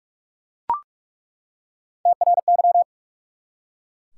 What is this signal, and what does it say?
Morse code, heard as a keyed tone, about 46 wpm, 700 Hz. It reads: TRY
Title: Three Letter Words - 45wpm